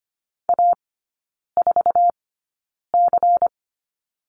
Morse code, heard as a keyed tone, about 25 wpm, spaced as for 9 wpm, 700 Hz. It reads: A4C
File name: course / 25:10wpm / 093 Lesson 16e - Letter c - Three Characters 25wpm